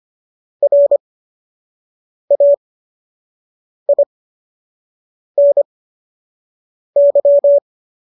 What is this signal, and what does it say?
Morse code, heard as a keyed tone, about 25 wpm, spaced as for 6 wpm, 570 Hz. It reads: RAINY